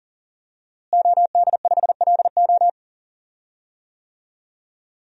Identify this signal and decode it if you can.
Morse code, heard as a keyed tone, about 40 wpm, 700 Hz. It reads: OD5LO
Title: DXCC-100 Call Signs - 5 Characters - Repeated 3x - 40wpm